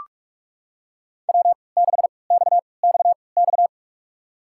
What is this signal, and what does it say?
Morse code, heard as a keyed tone, about 46 wpm, spaced as for 26 wpm, 700 Hz. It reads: W6XXX
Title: USA Call Signs - 5 Characters - 3x Character Spacing - 45wpm